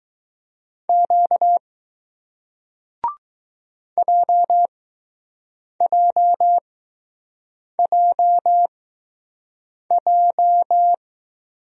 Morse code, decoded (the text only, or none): Q JJJJ